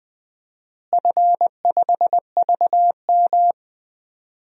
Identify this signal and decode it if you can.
Morse code, heard as a keyed tone, about 20 wpm, 700 Hz. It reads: F5VM
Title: DXCC-100 Call Signs - 4 Characters - Repeated 2x - 20wpm